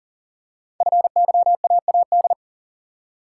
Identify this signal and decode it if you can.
Morse code, heard as a keyed tone, about 39 wpm, 700 Hz. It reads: FYAAD
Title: Code Groups - 5 Letters - 40wpm